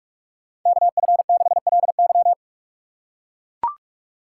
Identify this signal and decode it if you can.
Morse code, heard as a keyed tone, about 44 wpm, 690 Hz. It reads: KF6LY
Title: USA Call Signs - 5 Characters - 6x Character Spacing - 45wpm